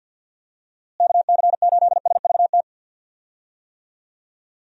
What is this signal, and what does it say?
Morse code, heard as a keyed tone, about 50 wpm, 700 Hz. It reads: KC8SVT
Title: USA Call Signs - 6 Characters - Repeated 3x - 50wpm